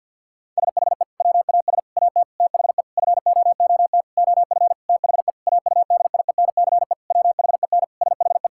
Morse code, heard as a keyed tone, about 51 wpm, 700 Hz. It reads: SHE WAS AT THE FOOT OF THE AUDIENCE WHEN SHE
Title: Sentences from Top 1000 Words - Review Part 5 - 50wpm